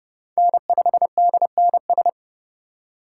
Morse code, heard as a keyed tone, about 30 wpm, 700 Hz. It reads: N5DNS
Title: USA Call Signs - 4x Character Spacing - 30wpm